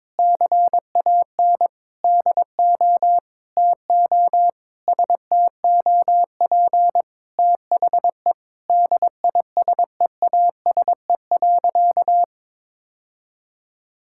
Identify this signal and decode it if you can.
Morse code, heard as a keyed tone, about 21 wpm, 700 Hz. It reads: CAN DO TO STOP THE DISEASE.
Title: Sentences from Top 1000 Words - Review Part 6 - 22wpm